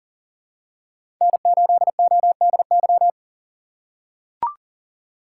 Morse code, heard as a keyed tone, about 40 wpm, 700 Hz. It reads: N8ODY
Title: USA Call Signs - 5 Characters - 4x Character Spacing - 40wpm